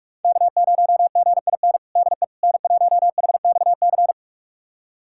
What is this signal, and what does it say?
Morse code, heard as a keyed tone, about 46 wpm, 700 Hz. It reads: K0GIN DE N1HXC
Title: USA Callsign DE Callsign - 5 Characters - 45wpm